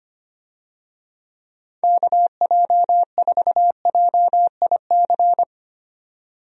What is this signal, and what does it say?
Morse code, heard as a keyed tone, about 25 wpm, 700 Hz. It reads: KJ4JIC
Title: USA Call Signs - 6 Characters - 25wpm